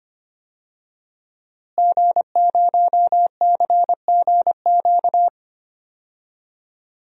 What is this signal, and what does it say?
Morse code, heard as a keyed tone, about 25 wpm, 700 Hz. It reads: G0CGQ
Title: DXCC-100 Call Signs - 5 Characters - Repeated 3x - 25wpm